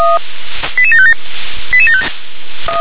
Five-tone selective call (ZVEI), heard as quick runs of five tones, tones of about 70 ms. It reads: 88757